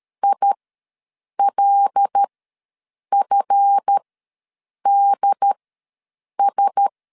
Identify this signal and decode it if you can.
Morse code, heard as a keyed tone, about 13 wpm, spaced as for 7 wpm, 800 Hz. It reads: ILFDS
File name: LEZIONI / 7 GF9Z / 7 GF ws 12wpm 3140